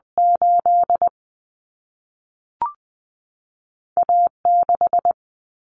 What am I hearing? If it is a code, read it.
Morse code, heard as a keyed tone, about 20 wpm, spaced as for 3 wpm, 700 Hz. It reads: 8A 6
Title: Common Cut Numbers - 2-Digit 20wpm